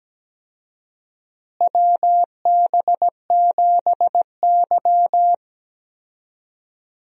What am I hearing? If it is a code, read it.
Morse code, heard as a keyed tone, about 17 wpm, 700 Hz. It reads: WB7Y
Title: USA Call Signs - 4 Characters - Repeated 2x - 17wpm